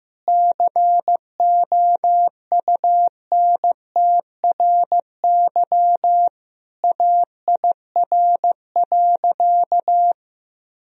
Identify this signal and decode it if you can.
Morse code, heard as a keyed tone, about 15 wpm, 700 Hz. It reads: COUNTRY AIR.
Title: Sentences from Top 1000 Words - Review Part 1 - 15wpm